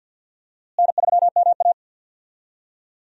Morse code, heard as a keyed tone, about 49 wpm, 700 Hz. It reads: N3MA
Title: USA Call Signs - 4 Characters - Repeated 3x - 50wpm